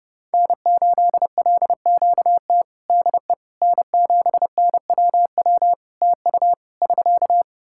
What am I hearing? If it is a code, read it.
Morse code, heard as a keyed tone, about 29 wpm, 700 Hz. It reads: N8LQT DE N7NWW TU <SK>
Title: Practice QSO - Exchange 4 - Ending - 30wpm